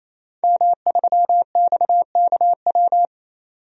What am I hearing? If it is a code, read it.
Morse code, heard as a keyed tone, about 28 wpm, 700 Hz. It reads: M3XKW
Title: DXCC-100 Call Signs - 5 Characters - 28wpm